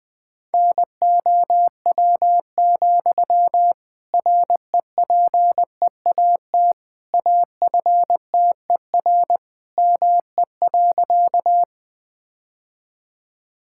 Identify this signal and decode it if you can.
Morse code, heard as a keyed tone, about 20 wpm, 700 Hz. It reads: NOW, REPEAT AFTER ME.
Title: Sentences from Top 1600 Words 20wpm